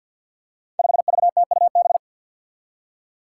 Morse code, heard as a keyed tone, about 49 wpm, 700 Hz. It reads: 5VTUB
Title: Five Challenging Letter-Number Combinations 50wpm